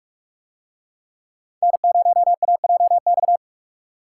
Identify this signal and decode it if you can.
Morse code, heard as a keyed tone, about 44 wpm, 700 Hz. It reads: N0AJX